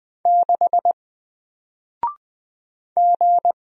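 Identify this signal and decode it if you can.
Morse code, heard as a keyed tone, about 20 wpm, 700 Hz. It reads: 6 G
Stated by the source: Single letter and number Morse drill at 20 wpm, from N4LCD.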